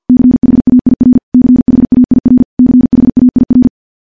Morse code, heard as a keyed tone, about 50 wpm, 250 Hz. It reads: CHAIR CHAIR CHAIR
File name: morse / morse-training-50WPM